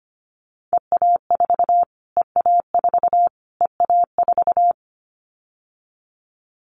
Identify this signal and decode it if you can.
Morse code, heard as a keyed tone, about 25 wpm, 700 Hz. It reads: EA4 EA4 EA4
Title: Callsign Prefixes for Top 25 DXCC - Repeated 3x - 25wpm